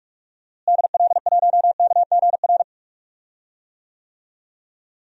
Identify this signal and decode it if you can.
Morse code, heard as a keyed tone, about 47 wpm, 700 Hz. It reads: DL1KGR
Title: DXCC-100 Call Signs - 6 Characters - Repeated 3x - 45wpm